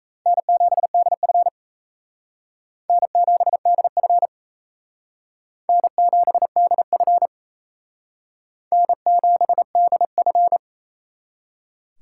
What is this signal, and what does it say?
Morse code, heard as a keyed tone, about 34 wpm, 700 Hz. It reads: N7DF N7DF N7DF N7DF
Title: Speed Racing - USA 1x2 Call Signs 28wpm